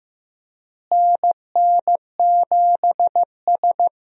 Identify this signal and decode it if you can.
Morse code, heard as a keyed tone, about 15 wpm, 700 Hz. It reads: NN7S
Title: USA Call Signs - 4 Characters - Repeated 2x - 15wpm